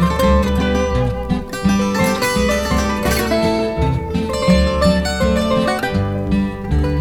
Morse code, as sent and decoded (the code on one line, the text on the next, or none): none